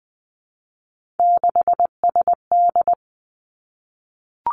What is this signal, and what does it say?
Morse code, heard as a keyed tone, about 19 wpm, 700 Hz. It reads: 6SD
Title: Three Challenging Letter-Number Combinations 20wpm